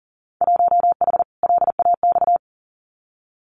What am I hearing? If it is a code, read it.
Morse code, heard as a keyed tone, about 40 wpm, 700 Hz. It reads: 1H LAX
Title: Winter Field Day Exchanges 40wpm